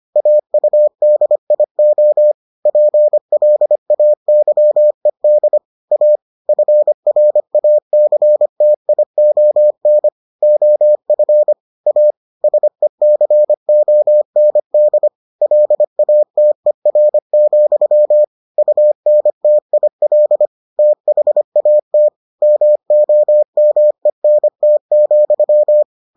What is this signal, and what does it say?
Morse code, heard as a keyed tone, about 25 wpm, 600 Hz. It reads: AUDIO PLAYED A FRACTION OF A SECOND LATER, UNTIL THAT MOMENT,